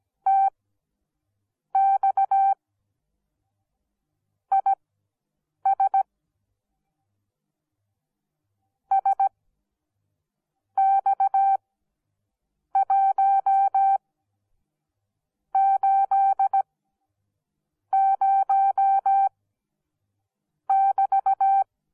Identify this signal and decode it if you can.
Morse code, heard as a keyed tone, about 17 wpm, spaced as for 6 wpm, 780 Hz. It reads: TX IS SX180<BT>